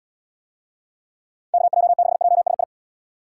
Morse code, heard as a keyed tone, about 78 wpm, 700 Hz. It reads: KJ6JSI